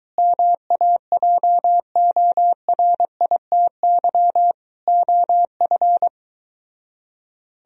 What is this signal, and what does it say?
Morse code, heard as a keyed tone, about 23 wpm, 700 Hz. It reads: MAJORITY OF